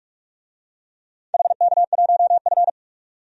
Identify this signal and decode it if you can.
Morse code, heard as a keyed tone, about 47 wpm, 700 Hz. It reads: HK1F
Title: DXCC-100 Call Signs - 4 Characters - 45wpm